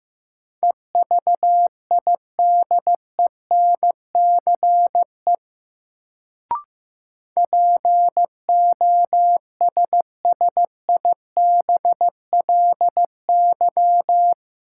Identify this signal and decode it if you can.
Morse code, heard as a keyed tone, about 15 wpm, 700 Hz. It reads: EVIDENCE POSSIBLY
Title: Sentences from Top 1300 Words - New Word Drill - 15wpm